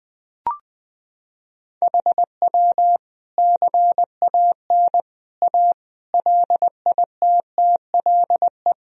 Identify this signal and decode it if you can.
Morse code, heard as a keyed tone, about 20 wpm, 700 Hz. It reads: HW CAN A LITTLE
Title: Sentences from Top 500 Words and Abbreviations 20wpm